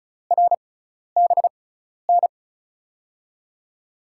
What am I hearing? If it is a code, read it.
Morse code, heard as a keyed tone, about 36 wpm, spaced as for 12 wpm, 700 Hz. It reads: RBN
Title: USA Call Signs - 5 Characters - 6x Character Spacing - 35wpm